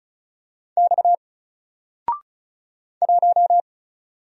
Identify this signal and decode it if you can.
Morse code, heard as a keyed tone, about 36 wpm, 700 Hz. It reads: X 1
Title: Single Letter-Number Alt 35wpm